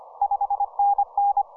Morse code, heard as a keyed tone, about 25 wpm, 810 Hz. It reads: 5NN
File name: simple-file-list / VE7SV 10m